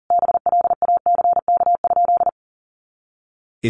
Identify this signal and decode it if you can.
Morse code, heard as a keyed tone, about 40 wpm, 700 Hz. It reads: BLACK?
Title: Sentences from Top 400 Words 40wpm